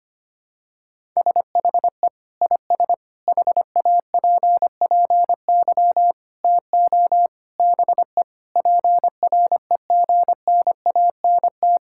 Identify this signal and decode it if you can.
Morse code, heard as a keyed tone, about 25 wpm, 700 Hz. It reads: SHE IS HAPPY TO BE PREGNANT
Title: Sentences from Top 1000 Words - Review Part 6 - 25wpm